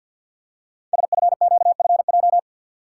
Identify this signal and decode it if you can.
Morse code, heard as a keyed tone, about 48 wpm, 700 Hz. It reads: SFQFJ
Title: Code Groups - 5 Letters - 50wpm